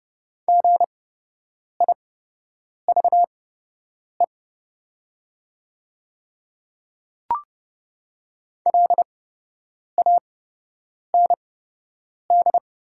Morse code, heard as a keyed tone, about 29 wpm, spaced as for 8 wpm, 700 Hz. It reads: GIVE LAND
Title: 8x Character Spacing - Top 200 Words - 30wpm